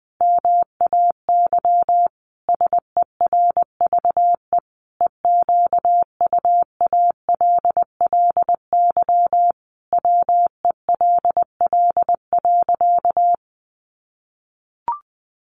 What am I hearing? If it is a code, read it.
Morse code, heard as a keyed tone, about 20 wpm, 700 Hz. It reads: MAY SERVE EQUALLY WELL.
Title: Sentences from Top 1400 Words 20wpm